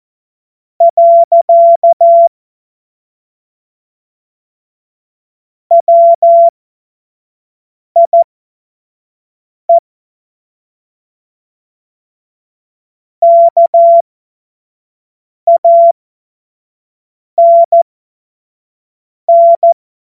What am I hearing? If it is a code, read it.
Morse code, heard as a keyed tone, about 14 wpm, spaced as for 5 wpm, 680 Hz. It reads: . WIE KANN
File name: Morsekurs / Witze / Witz 17. August 2021, 25 F70